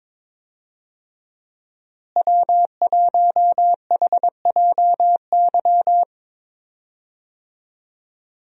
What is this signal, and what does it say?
Morse code, heard as a keyed tone, about 22 wpm, 700 Hz. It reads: W1HJY
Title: USA Call Signs - 5 Characters - Repeated 2x - 22wpm